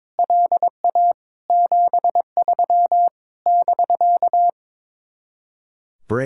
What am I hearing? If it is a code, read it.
Morse code, heard as a keyed tone, about 22 wpm, 700 Hz. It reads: LA 73 <BK>
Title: Speed-Racing Common QSO Phrases - 22wpm